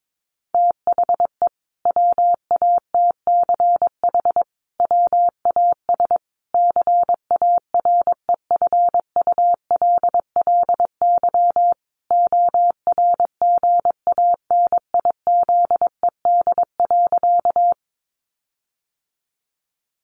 Morse code, heard as a keyed tone, about 22 wpm, 700 Hz. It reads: THE WATCH WAS CAREFULLY ORGANIZED.
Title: Sentences from Top 1300 Words 22wpm